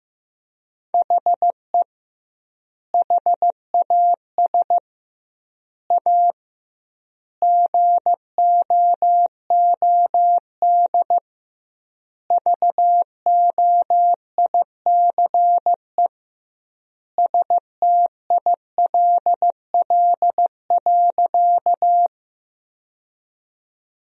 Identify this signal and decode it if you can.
Morse code, heard as a keyed tone, about 16 wpm, 700 Hz. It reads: HE HAS A GOOD VOICE STILL.
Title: 2x Word-Spacing - Sentences from Top 500 Words - 15wpm